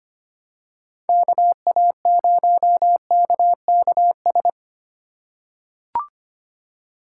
Morse code, heard as a keyed tone, about 25 wpm, 700 Hz. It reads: KA0KKS
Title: USA Call Signs - 6 Characters - 3x Character Spacing - 25wpm